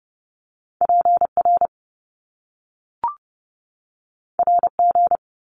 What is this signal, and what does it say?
Morse code, heard as a keyed tone, about 30 wpm, 700 Hz. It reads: P R R G
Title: Top Two Letter Combinations in Words 30wpm